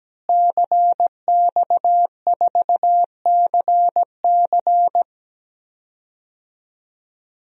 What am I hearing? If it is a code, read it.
Morse code, heard as a keyed tone, about 17 wpm, 700 Hz. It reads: CX4CC
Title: DXCC-100 Call Signs - 5 Characters - Repeated 2x - 17wpm